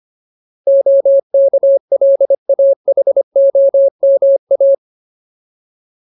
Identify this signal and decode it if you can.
Morse code, heard as a keyed tone, about 25 wpm, 550 Hz. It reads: OKLAHOMA
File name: Ultimate Word List / 25WPM Ultimate